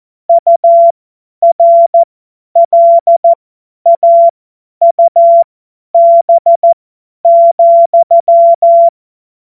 Morse code, heard as a keyed tone, about 14 wpm, spaced as for 10 wpm, 680 Hz. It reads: URLAUB,
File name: Witz-12-August-2025-50-F70